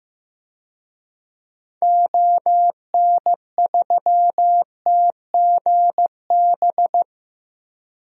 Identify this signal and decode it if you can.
Morse code, heard as a keyed tone, about 15 wpm, 700 Hz. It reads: ON3TGB
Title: DXCC-100 Call Signs - 6 Characters - Repeated 2x - 15wpm